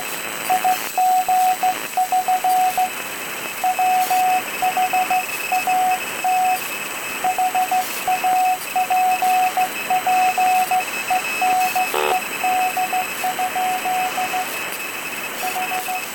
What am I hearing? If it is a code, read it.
Morse code, heard as a keyed tone, about 15 wpm, 740 Hz, with noise. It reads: IG<SN> WHAT HAPPENED? H